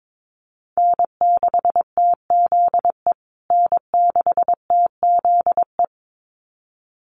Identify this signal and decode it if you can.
Morse code, heard as a keyed tone, about 22 wpm, 700 Hz. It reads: N6TZE N6TZE
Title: USA Call Signs Repeated 2x 22wpm